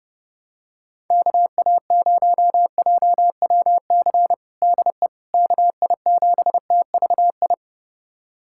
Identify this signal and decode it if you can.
Morse code, heard as a keyed tone, about 32 wpm, 700 Hz. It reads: KA0JWC DE KI7TVI
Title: USA Callsign DE Callsign - 6 Characters - 30wpm